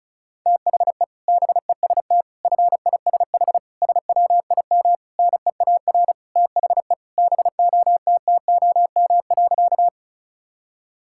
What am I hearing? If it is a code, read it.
Morse code, heard as a keyed tone, about 34 wpm, 690 Hz. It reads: THE BEST FISH SWIM NEAR THE BOTTOM.
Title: Sentences from Top 1000 Words - Review Part 6 - 35wpm